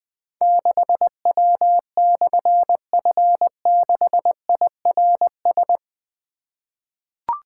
Morse code, heard as a keyed tone, about 21 wpm, 700 Hz. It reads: 6W/F6IRS
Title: Difficult Call Signs 20wpm